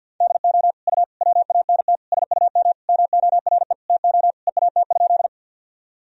Morse code, heard as a keyed tone, about 52 wpm, 700 Hz. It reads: DO U WANT SUM MORE TO EAT?
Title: Sentences from Top 300 Words and Abbreviations 50wpm